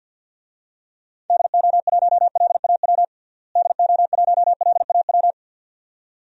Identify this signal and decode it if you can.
Morse code, heard as a keyed tone, about 52 wpm, 700 Hz. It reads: DO1LAW DO1LAW
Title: DXCC-100 Call Signs - 6 Characters - Repeated 2x - 50wpm